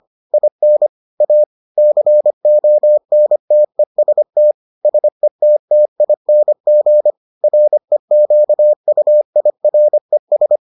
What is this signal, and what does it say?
Morse code, heard as a keyed tone, about 25 wpm, 600 Hz. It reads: IN A CONTEST SETTING REQUIRES